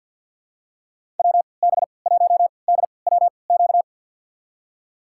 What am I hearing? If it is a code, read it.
Morse code, heard as a keyed tone, about 49 wpm, spaced as for 28 wpm, 700 Hz. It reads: WB1DWQ
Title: USA Call Signs - 6 Characters - 3x Character Spacing - 50wpm